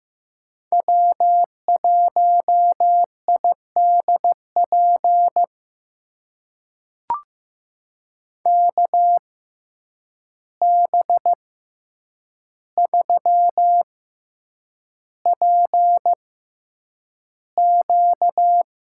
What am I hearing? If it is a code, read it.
Morse code, heard as a keyed tone, about 16 wpm, spaced as for 8 wpm, 700 Hz. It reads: W1IDP KB3PQ
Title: USA Call Signs - 6x Character Spacing - 15wpm